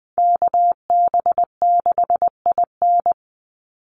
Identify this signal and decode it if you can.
Morse code, heard as a keyed tone, about 20 wpm, 700 Hz. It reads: KB6IN
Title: USA Call Signs 20wpm